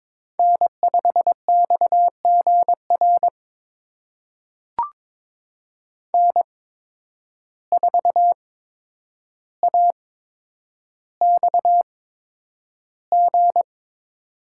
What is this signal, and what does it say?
Morse code, heard as a keyed tone, about 21 wpm, spaced as for 9 wpm, 700 Hz. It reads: N5XGR N4AXG
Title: USA Call Signs - 5 Characters - 8x Character Spacing - 22wpm